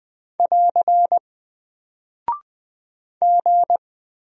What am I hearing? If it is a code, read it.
Morse code, heard as a keyed tone, about 20 wpm, 700 Hz. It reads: <AR> G
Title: Single Character (Letter:Number:Punctuation:Prosign) Alt 20wpm